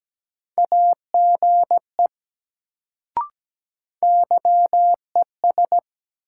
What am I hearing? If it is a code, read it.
Morse code, heard as a keyed tone, about 17 wpm, 700 Hz. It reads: AGE YES
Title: Sentences from Top 500 Words - New Word Drill - 17wpm